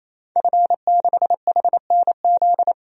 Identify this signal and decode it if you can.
Morse code, heard as a keyed tone, about 28 wpm, 700 Hz. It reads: F6HNZ
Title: DXCC-100 Call Signs - 5 Characters - 28wpm